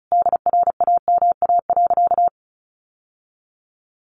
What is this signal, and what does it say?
Morse code, heard as a keyed tone, about 34 wpm, 700 Hz. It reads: DRAMA.
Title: Sentences from Top 1500 Words 35wpm